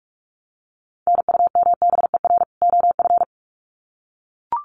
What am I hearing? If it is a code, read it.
Morse code, heard as a keyed tone, about 44 wpm, 700 Hz. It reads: NUMBER OF
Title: Sets of 2 Words 45wpm